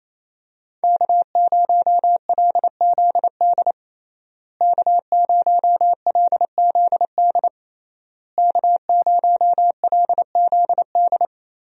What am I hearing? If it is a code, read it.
Morse code, heard as a keyed tone, about 28 wpm, 700 Hz. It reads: K0LZD K0LZD K0LZD
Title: USA Call Signs - 5 Characters - Repeated 3x - 28wpm